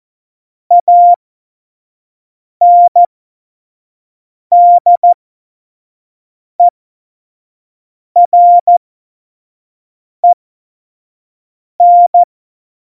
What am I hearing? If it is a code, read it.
Morse code, heard as a keyed tone, about 14 wpm, spaced as for 5 wpm, 700 Hz. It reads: ANDEREN
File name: Morsekurs / Witze / Witz 27. November 2018, 25 F70